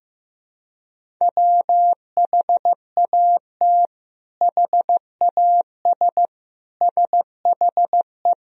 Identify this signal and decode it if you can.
Morse code, heard as a keyed tone, about 15 wpm, 700 Hz. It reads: WHAT HAS SHE
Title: Sentences from Top 100 Words and Abbreviations 15wpm